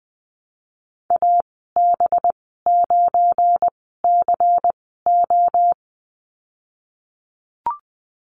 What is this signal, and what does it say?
Morse code, heard as a keyed tone, about 20 wpm, spaced as for 15 wpm, 700 Hz. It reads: AB9CO
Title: 193 Lesson 31g - Letter v - Call Signs 20wpm